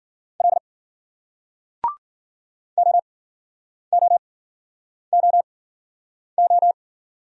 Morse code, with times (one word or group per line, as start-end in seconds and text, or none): 0.4-0.6 L
2.8-6.7 OOOO